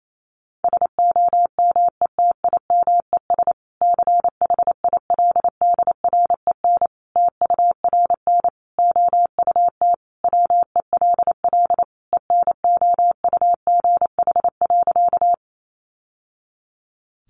Morse code, heard as a keyed tone, about 28 wpm, 700 Hz. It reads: SOMETIMES CHILDREN TURN OUT WELL ENOUGH.
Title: Sentences from Top 600 Words 28wpm